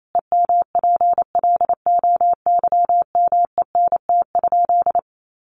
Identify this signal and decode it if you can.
Morse code, heard as a keyed tone, about 28 wpm, 700 Hz. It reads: EMPLOYMENT?
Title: Sentences from Top 1500 Words 28wpm